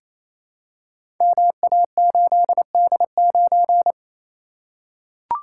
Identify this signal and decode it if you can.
Morse code, heard as a keyed tone, about 29 wpm, 700 Hz. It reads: MA8D9
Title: Code Groups - 5 Letter:Numbers - 28wpm